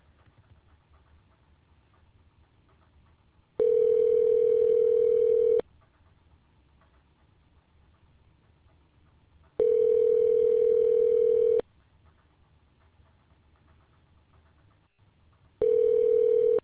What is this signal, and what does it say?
Ringback tone (North America), 440 + 480 Hz.